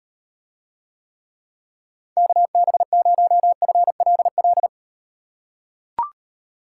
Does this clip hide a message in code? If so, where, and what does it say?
Yes: Morse code from 2.2 to 4.7 s, heard as a keyed tone, about 36 wpm, 700 Hz. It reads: KB0FLL